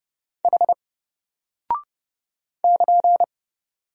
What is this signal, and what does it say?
Morse code, heard as a keyed tone, about 31 wpm, 700 Hz. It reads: H <KN>
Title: Single Character (Letter:Number:Punctuation:Prosign) Alt 30wpm